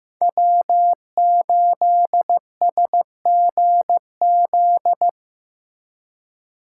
Morse code, heard as a keyed tone, about 15 wpm, 700 Hz. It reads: W8SGZ
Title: USA Call Signs - 4x Character Spacing - 15wpm